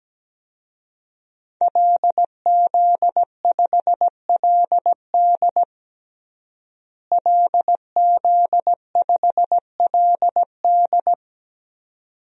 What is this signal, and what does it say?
Morse code, heard as a keyed tone, about 17 wpm, 700 Hz. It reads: LZ5LD LZ5LD
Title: DXCC-100 Call Signs - 5 Characters - Repeated 2x - 17wpm